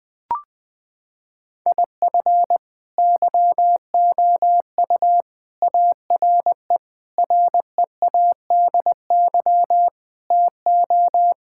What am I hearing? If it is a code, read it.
Morse code, heard as a keyed tone, about 20 wpm, 700 Hz. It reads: IF YOU ARE READY TO
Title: Sentences with 6-Word Phrases - 20wpm